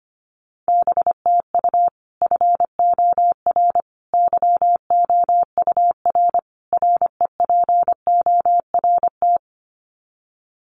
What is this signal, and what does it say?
Morse code, heard as a keyed tone, about 25 wpm, 700 Hz. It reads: BTU FOR YOUR REPORT
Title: Common QSO Phrases 25wpm